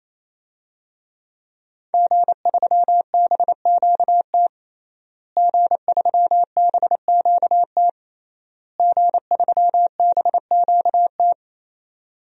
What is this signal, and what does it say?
Morse code, heard as a keyed tone, about 28 wpm, 700 Hz. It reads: G3BQT G3BQT G3BQT
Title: DXCC-100 Call Signs - 5 Characters - Repeated 3x - 28wpm